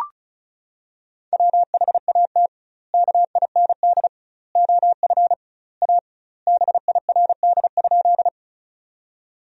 Morse code, heard as a keyed tone, about 36 wpm, 700 Hz. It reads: WHAT KIND OF A BIRD?
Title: 2x Word-Spacing - Sentences from Top 400 Words - 35wpm